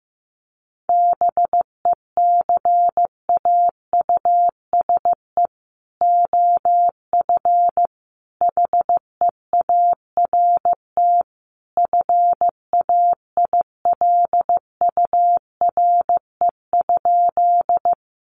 Morse code, heard as a keyed tone, about 15 wpm, 700 Hz. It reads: BECAUSE OF HEART FAILURE?
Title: Sentences from Top 1200 Words 15wpm